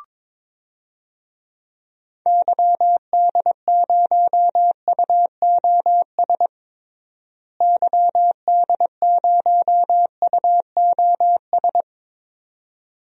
Morse code, heard as a keyed tone, about 22 wpm, 700 Hz. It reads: YD0UOS YD0UOS
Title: DXCC-100 Call Signs - 6 Characters - Repeated 2x - 22wpm